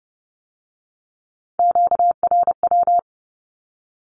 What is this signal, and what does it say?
Morse code, heard as a keyed tone, about 30 wpm, 700 Hz. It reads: QRW
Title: Q-Codes with Spoken Meaning - 30wpm